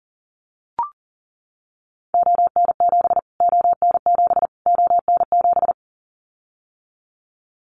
Morse code, heard as a keyed tone, about 40 wpm, 700 Hz. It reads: ON7 ON7 ON7
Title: Callsign Prefixes for Top 25 DXCC - Repeated 3x - 40wpm